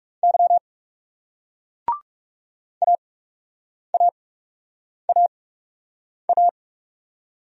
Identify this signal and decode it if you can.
Morse code, heard as a keyed tone, about 40 wpm, spaced as for 8 wpm, 700 Hz. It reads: Y AAAA